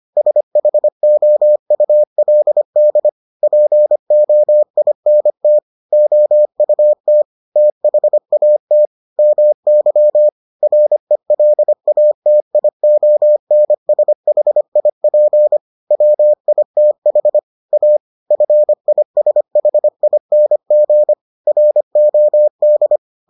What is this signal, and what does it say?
Morse code, heard as a keyed tone, about 25 wpm, 600 Hz. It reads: SHOULD POINT OUT THAT MY RELATIONSHIP WITH A FISHING ROD